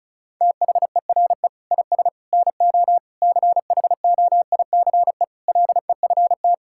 Morse code, heard as a keyed tone, about 36 wpm, 700 Hz. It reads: THERE IS NO CHOICE LEFT
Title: Sentences from Top 1000 Words - Review Part 6 - 35wpm